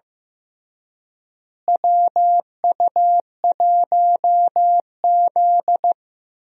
Morse code, heard as a keyed tone, about 16 wpm, 700 Hz. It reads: WU1Z
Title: USA Call Signs - 4 Characters - Repeated 3x - 15wpm